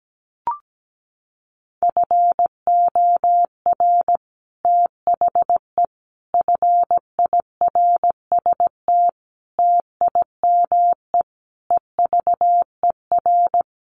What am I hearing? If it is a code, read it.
Morse code, heard as a keyed tone, about 17 wpm, 700 Hz. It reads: FOR THE FIRST TIME EVER
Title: Sets of 5 Words Encore2 17wpm